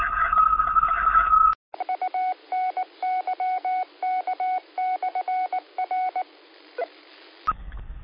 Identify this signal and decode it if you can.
Morse code, heard as a keyed tone, about 19 wpm, 740 Hz. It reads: UNYK/R
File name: qsorec SimplexLogic